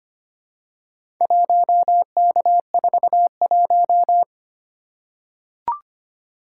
Morse code, heard as a keyed tone, about 25 wpm, 700 Hz. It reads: 1K41
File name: practice / 25wpm / Four Letter-Number Combinations - 25wpm